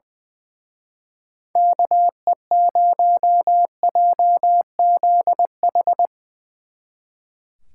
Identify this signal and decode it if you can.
Morse code, heard as a keyed tone, about 24 wpm, 700 Hz. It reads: KE0JZH
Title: Speed Racing - USA 2x3 Call Signs 20wpm